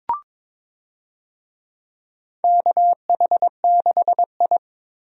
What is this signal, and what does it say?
Morse code, heard as a keyed tone, about 23 wpm, 700 Hz. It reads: KH6I
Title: DXCC-100 Call Signs - 4 Characters - 22wpm